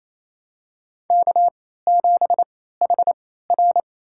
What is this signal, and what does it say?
Morse code, heard as a keyed tone, about 28 wpm, spaced as for 16 wpm, 700 Hz. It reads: K7HR
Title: USA Call Signs - 4 Characters - 3x Character Spacing - 28wpm